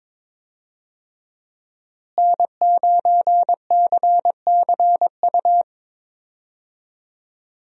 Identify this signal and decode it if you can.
Morse code, heard as a keyed tone, about 22 wpm, 700 Hz. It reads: N9CCU
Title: USA Call Signs - 5 Characters - Repeated 3x - 22wpm